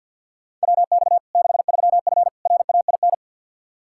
Morse code, heard as a keyed tone, about 53 wpm, 700 Hz. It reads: WX 63F RAIN
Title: Practice QSO - Exchange 2 - Info - 50wpm